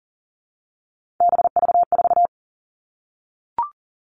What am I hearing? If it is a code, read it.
Morse code, heard as a keyed tone, about 40 wpm, 700 Hz. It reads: BV4
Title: All Cut Numbers - 3-Digit 40wpm